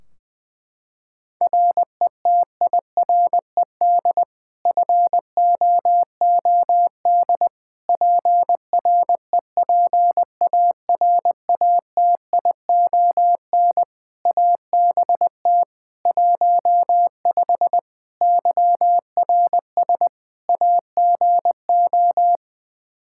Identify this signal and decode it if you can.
Morse code, heard as a keyed tone, about 21 wpm, 700 Hz. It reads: RETIRED FOOD PREPARATION ABT 15 YRS AGO
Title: K1USN SST QSO Practice 20wpm